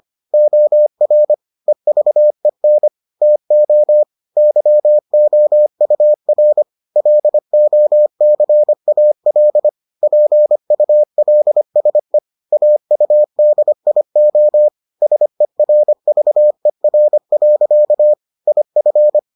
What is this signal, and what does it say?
Morse code, heard as a keyed tone, about 25 wpm, 600 Hz. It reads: OR EVEN TO YOUR LOCAL PULSE AUDIO SERVER. IF